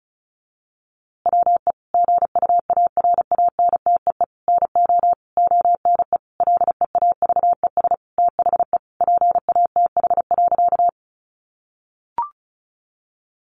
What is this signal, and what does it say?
Morse code, heard as a keyed tone, about 34 wpm, 700 Hz. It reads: WE GUARANTEE NO ONE LEAVES THE PATH.
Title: A Sound Of Thunder 35wpm